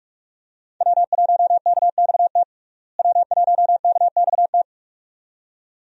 Morse code, heard as a keyed tone, about 46 wpm, 700 Hz. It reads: W1KXT W1KXT
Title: USA Call Signs - 5 Characters - Repeated 2x - 45wpm